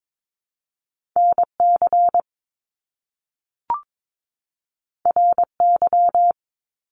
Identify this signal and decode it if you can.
Morse code, heard as a keyed tone, about 22 wpm, 700 Hz. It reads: NC RY
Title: Top Two Letter Combinations in Words 22wpm